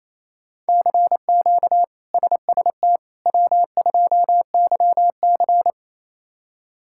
Morse code, heard as a keyed tone, about 28 wpm, 700 Hz. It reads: CQ SST W2YC
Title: K1USN SST QSO Practice - 28wpm